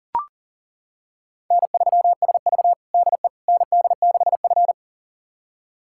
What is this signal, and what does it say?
Morse code, heard as a keyed tone, about 40 wpm, 700 Hz. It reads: N3SV DE ND6F
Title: USA Callsign DE Callsign - 4 Characters - 40wpm